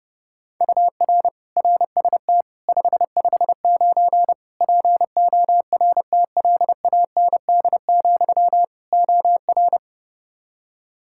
Morse code, heard as a keyed tone, about 30 wpm, 700 Hz. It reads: UR RST 559 PORTLAND, OR
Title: K1USN SST QSO Practice 30wpm